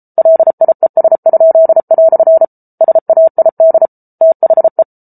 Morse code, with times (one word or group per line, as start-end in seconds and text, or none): 0.2-2.5 LIES?"
2.8-3.9 SAID
4.2-4.8 THE